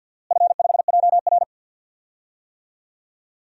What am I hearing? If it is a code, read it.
Morse code, heard as a keyed tone, about 47 wpm, 700 Hz. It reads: F5JR
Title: DXCC-100 Call Signs - 4 Characters - Repeated 3x - 50wpm